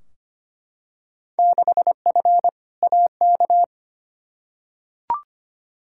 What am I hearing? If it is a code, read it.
Morse code, heard as a keyed tone, about 25 wpm, 700 Hz. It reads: 6F AK
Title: Field Day Exchanges Encore - 25wpm